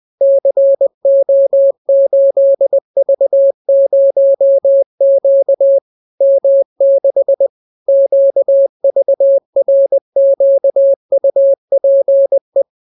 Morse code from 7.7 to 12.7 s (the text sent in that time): QVRQUPE